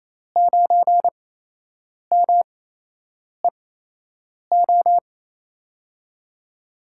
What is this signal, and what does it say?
Morse code, heard as a keyed tone, about 28 wpm, spaced as for 8 wpm, 700 Hz. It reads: 9MEO
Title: USA Call Signs - 5 Characters - 8x Character Spacing - 28wpm